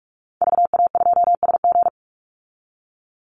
Morse code, heard as a keyed tone, about 44 wpm, 700 Hz. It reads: VA2SG
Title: Call Signs Repeated 2x - 45wpm